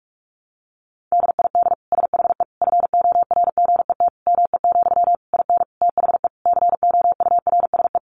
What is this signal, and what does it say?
Morse code, heard as a keyed tone, about 44 wpm, 700 Hz. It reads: DID SHE FORGET ME, IN THE COURSE
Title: Sentences from Top 1200 Words 45wpm